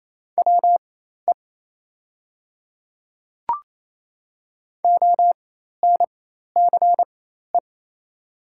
Morse code, heard as a keyed tone, about 27 wpm, spaced as for 13 wpm, 700 Hz. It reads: WE ONCE
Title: 4x Character Spacing - Top 300 Words - 28wpm